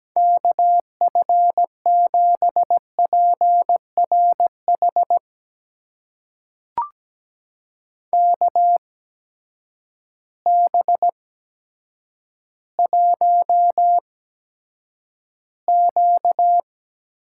Morse code, heard as a keyed tone, about 17 wpm, spaced as for 8 wpm, 700 Hz. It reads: KF7PRH KB1Q
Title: USA Call Signs - 6 Characters - 8x Character Spacing - 17wpm